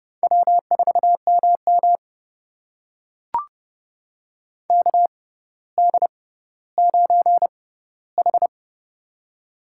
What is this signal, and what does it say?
Morse code, heard as a keyed tone, about 30 wpm, spaced as for 15 wpm, 700 Hz. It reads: W4MM KD9H
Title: USA Call Signs - 4 Characters - 6x Character Spacing - 30wpm